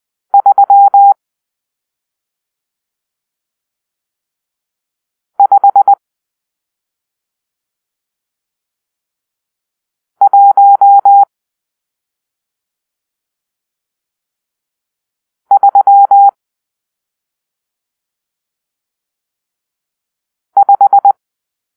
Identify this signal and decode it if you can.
Morse code, heard as a keyed tone, about 20 wpm, 800 Hz. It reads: 3 5 1 3 5
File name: arrl / Grp-6 Drill-1